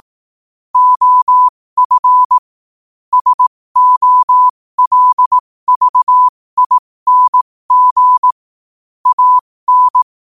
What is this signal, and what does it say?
Morse code, heard as a keyed tone, about 18 wpm, spaced as for 15 wpm, 1000 Hz. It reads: OF SOLVING AN
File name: qotdfiles / 15wpm